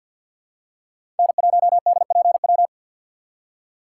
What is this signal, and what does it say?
Morse code, heard as a keyed tone, about 50 wpm, 690 Hz. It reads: N1DPW
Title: USA Call Signs - 5 Characters - 6x Character Spacing - 50wpm